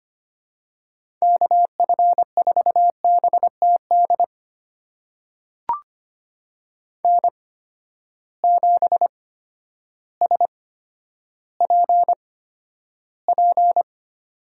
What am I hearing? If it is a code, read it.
Morse code, heard as a keyed tone, about 25 wpm, spaced as for 11 wpm, 700 Hz. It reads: KF4BTD N7SPP
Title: USA Call Signs - 8x Character Spacing - 25wpm